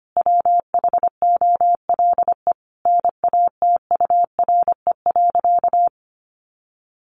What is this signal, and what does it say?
Morse code, heard as a keyed tone, about 25 wpm, 700 Hz. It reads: WHOLE NATURE.
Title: Sentences from Top 1500 Words 25wpm